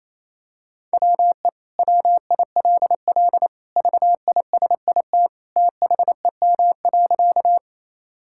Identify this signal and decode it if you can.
Morse code, heard as a keyed tone, about 28 wpm, 700 Hz. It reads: WE WILL VISIT THEM.
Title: Sentences from Top 1000 Words - Review Part 6 - 28wpm